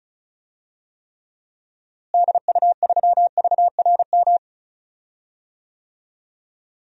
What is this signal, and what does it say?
Morse code, heard as a keyed tone, about 36 wpm, 700 Hz. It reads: DU3VRM
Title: DXCC-100 Call Signs - 6 Characters - Repeated 3x - 35wpm